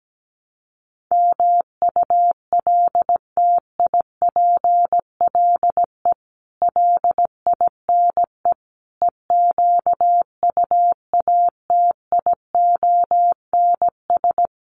Morse code, heard as a keyed tone, about 17 wpm, 700 Hz. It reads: MULTIPLE LINE EQUATIONS
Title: Sets of 5 Words 17wpm